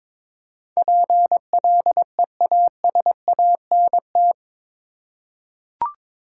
Morse code, heard as a keyed tone, about 21 wpm, 700 Hz. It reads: PLEASANT
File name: practice / 22wpm / Eight Letter Words - 22wpm